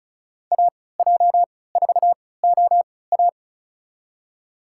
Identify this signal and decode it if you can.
Morse code, heard as a keyed tone, about 34 wpm, spaced as for 20 wpm, 700 Hz. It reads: AJ4OA
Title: USA Call Signs - 5 Characters - 3x Character Spacing - 35wpm